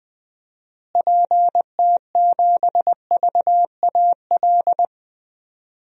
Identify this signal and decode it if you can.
Morse code, heard as a keyed tone, about 20 wpm, 700 Hz. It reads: PT7VAL
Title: DXCC-100 Call Signs - 6 Characters - Repeated 3x - 20wpm